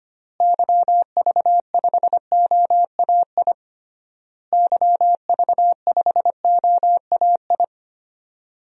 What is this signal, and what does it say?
Morse code, heard as a keyed tone, about 25 wpm, 700 Hz. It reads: YV5OAI YV5OAI
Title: DXCC-100 Call Signs - 6 Characters - Repeated 3x - 25wpm